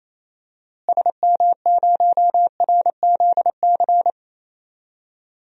Morse code, heard as a keyed tone, about 28 wpm, 700 Hz. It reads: SM0RZC
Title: DXCC-100 Call Signs - 6 Characters - Repeated 2x - 28wpm